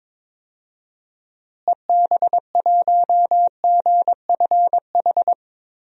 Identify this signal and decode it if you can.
Morse code, heard as a keyed tone, about 22 wpm, 700 Hz. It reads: EB1GFH
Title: DXCC-100 Call Signs - 6 Characters - Repeated 3x - 22wpm